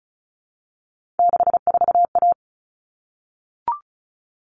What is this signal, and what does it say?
Morse code, heard as a keyed tone, about 34 wpm, 700 Hz. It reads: 64A
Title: All Cut Numbers - 3-Digit 35wpm